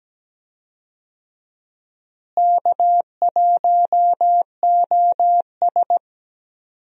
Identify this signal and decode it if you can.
Morse code, heard as a keyed tone, about 17 wpm, 700 Hz. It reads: K1OS
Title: USA Call Signs - 4 Characters - Repeated 3x - 17wpm